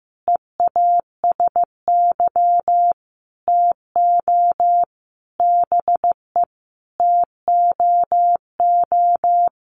Morse code, heard as a keyed tone, about 15 wpm, 700 Hz. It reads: EASY TO BE TOO
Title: Sentences from Top 1400 Words 15wpm